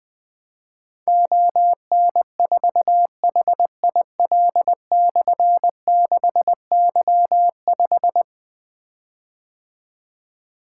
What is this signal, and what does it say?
Morse code, heard as a keyed tone, about 21 wpm, 700 Hz. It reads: ON4HIL/6Y5
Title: Difficult Call Signs 20wpm